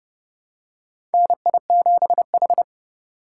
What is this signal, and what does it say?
Morse code, heard as a keyed tone, about 30 wpm, 700 Hz. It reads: NI7H
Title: USA Call Signs - 4 Characters - 2x Character Spacing - 30wpm